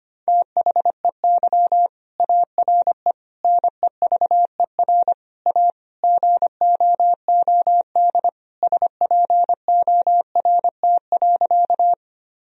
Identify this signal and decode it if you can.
Morse code, heard as a keyed tone, about 25 wpm, 700 Hz. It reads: THEY ARE NEVER A GOOD SPORT.
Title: Sentences from Top 1000 Words - Review Part 4 - 25wpm